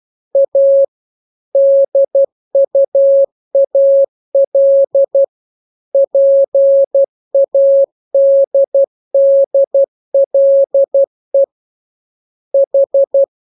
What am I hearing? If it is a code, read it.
Morse code, heard as a keyed tone, about 12 wpm, 550 Hz. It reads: A DUAL PADDLE H
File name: jokes / joke 1 12 0000